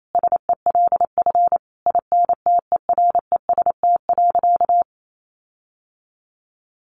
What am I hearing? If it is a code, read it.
Morse code, heard as a keyed tone, about 28 wpm, 700 Hz. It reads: SELF INTEREST.
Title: Sentences from Top 1400 Words 28wpm